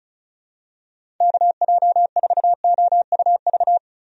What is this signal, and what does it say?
Morse code, heard as a keyed tone, about 34 wpm, 700 Hz. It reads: KJ4OUV
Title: USA Call Signs - 6 Characters - Repeated 3x - 35wpm